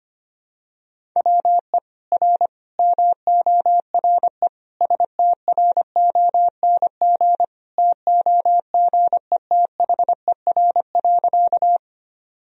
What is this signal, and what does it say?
Morse code, heard as a keyed tone, about 25 wpm, 700 Hz. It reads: WE R MORE STRONG TOGETHER.
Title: Sentences from Top 500 Words and Abbreviations 25wpm